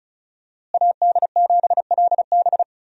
Morse code, heard as a keyed tone, about 36 wpm, 700 Hz. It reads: AD7LB
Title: USA Call Signs - 5 Characters - Repeated 3x - 35wpm